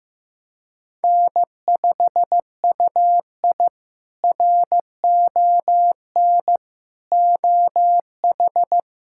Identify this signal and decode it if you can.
Morse code, heard as a keyed tone, about 15 wpm, 700 Hz. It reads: N5UI RON OH
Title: K1USN SST QSO Practice - 15wpm